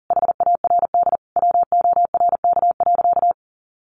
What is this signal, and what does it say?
Morse code, heard as a keyed tone, about 40 wpm, 700 Hz. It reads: HARD WORK.
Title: Sentences from Top 1400 Words 40wpm